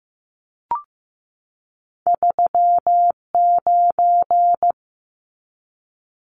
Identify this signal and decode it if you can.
Morse code, heard as a keyed tone, about 15 wpm, 700 Hz. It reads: 39
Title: Numbers - 2-Digit 15wpm